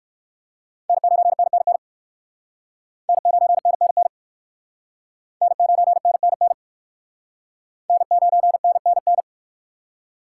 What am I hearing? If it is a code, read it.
Morse code, heard as a keyed tone, about 55 wpm, 700 Hz. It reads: N9NNN N9NNN N9NNN N9NNN
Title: Speed Racing - USA 1x3 Call Signs 45wpm